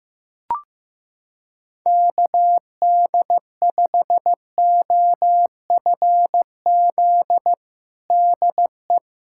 Morse code, heard as a keyed tone, about 15 wpm, 700 Hz. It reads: KD5OFZ DE
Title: USA Callsign DE Callsign - 6 Characters - 15wpm